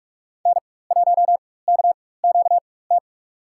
Morse code, heard as a keyed tone, about 46 wpm, spaced as for 21 wpm, 700 Hz. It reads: N1KQT
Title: USA Call Signs - 5 Characters - 4x Character Spacing - 45wpm